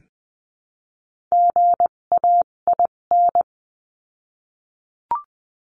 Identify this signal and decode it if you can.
Morse code, heard as a keyed tone, about 20 wpm, spaced as for 17 wpm, 700 Hz. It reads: GAIN
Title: 257 Lesson 42b - Words 14-20wpm